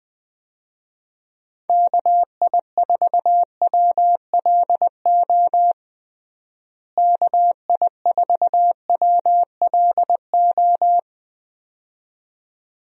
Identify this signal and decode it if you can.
Morse code, heard as a keyed tone, about 21 wpm, 700 Hz. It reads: KI4WLO KI4WLO
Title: USA Call Signs - 6 Characters - Repeated 2x - 20wpm